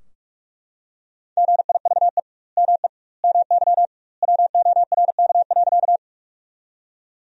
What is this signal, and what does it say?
Morse code, heard as a keyed tone, about 43 wpm, 700 Hz. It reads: GIVE ME MY WORK.
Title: 2x Word-Spacing - Sentences from Top 200 Words - 45wpm